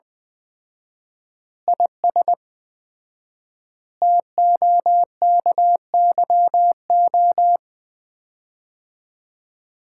Morse code, heard as a keyed tone, about 20 wpm, 700 Hz. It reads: IS TOKYO
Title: Common QSO Phrases - 4x Word-Spacing - 20wpm